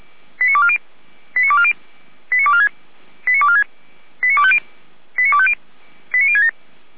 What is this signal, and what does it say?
Five-tone selective call (ZVEI), heard as quick runs of five tones, tones of about 70 ms. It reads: 89240 89240 89246 89246 89250 89250 89067
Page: /455/SDR/ZVEI/